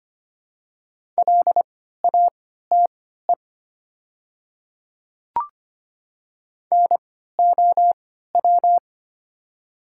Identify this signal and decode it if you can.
Morse code, heard as a keyed tone, about 25 wpm, spaced as for 14 wpm, 690 Hz. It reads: LATE NOW
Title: 3x Character Spacing - Top 300 Words - 25wpm